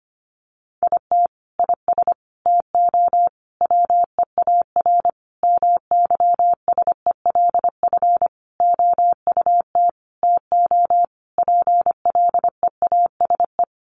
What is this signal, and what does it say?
Morse code, heard as a keyed tone, about 25 wpm, 700 Hz. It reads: IT IS TO WEAR MYSELF OUT TO PLEASE
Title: Sentences from Top 1500 Words 25wpm